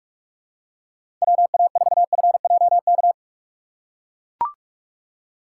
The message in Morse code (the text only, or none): WA4FJK